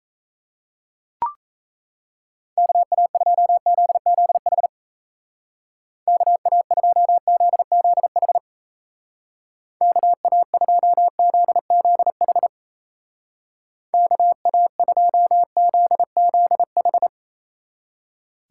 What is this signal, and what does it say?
Morse code, heard as a keyed tone, about 35 wpm, 700 Hz. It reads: KA2ZZH KA2ZZH KA2ZZH KA2ZZH
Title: Speed Racing - USA 2x3 Call Signs 28wpm